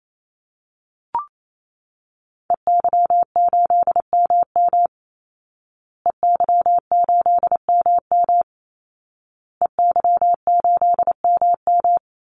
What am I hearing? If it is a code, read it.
Morse code, heard as a keyed tone, about 28 wpm, 700 Hz. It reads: EY8MM EY8MM EY8MM
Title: Call Signs Repeated 2x - 28wpm